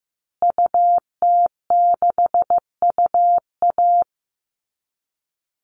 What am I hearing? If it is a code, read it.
Morse code, heard as a keyed tone, about 15 wpm, 700 Hz. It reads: UT6UA
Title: Call Signs Repeated 2x - 15wpm